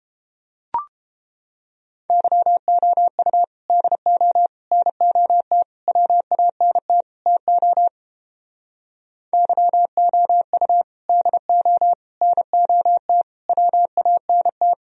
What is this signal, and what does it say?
Morse code, heard as a keyed tone, about 31 wpm, 700 Hz. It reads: YOU DO NOT WANT TO YOU DO NOT WANT